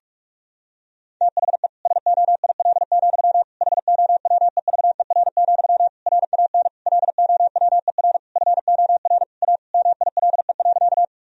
Morse code, heard as a keyed tone, about 41 wpm, 700 Hz. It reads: THE SOIL, HOWEVER, RAN LOWER FOR A MILE.
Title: Sentences from Top 1000 Words - Review Part 4 - 45wpm